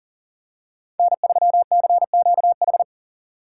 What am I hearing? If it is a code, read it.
Morse code, heard as a keyed tone, about 40 wpm, 700 Hz. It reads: N3CQH
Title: USA Call Signs - 6x Character Spacing - 40wpm